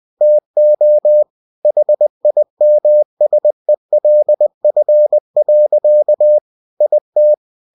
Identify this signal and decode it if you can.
Morse code, heard as a keyed tone, about 20 wpm, 600 Hz. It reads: TO HIMSELF. IT
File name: cw-books / winnie-the-pooh / winnie-20wpm-page-0003